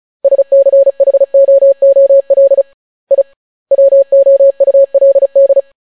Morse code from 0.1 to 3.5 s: SCHOOL I